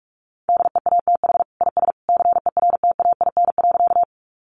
Morse code, heard as a keyed tone, about 44 wpm, 700 Hz. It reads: DEATH IS CERTAIN.
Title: Sentences from Top 1100 Words 45wpm